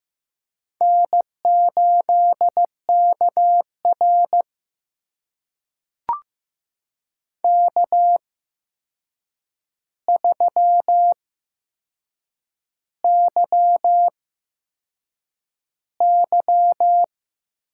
Morse code, heard as a keyed tone, about 15 wpm, spaced as for 6 wpm, 700 Hz. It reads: N8KR K3YY
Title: USA Call Signs - 8x Character Spacing - 15wpm